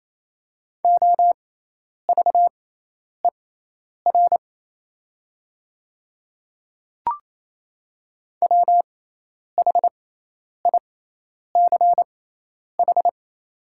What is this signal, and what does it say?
Morse code, heard as a keyed tone, about 28 wpm, spaced as for 10 wpm, 700 Hz. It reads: OVER WHICH
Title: 6x Character Spacing - Top 100 Words - 28wpm